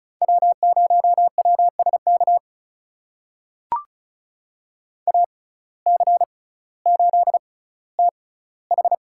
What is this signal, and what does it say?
Morse code, heard as a keyed tone, about 36 wpm, spaced as for 18 wpm, 700 Hz. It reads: W0WSK AC8TH
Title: USA Call Signs - 6x Character Spacing - 35wpm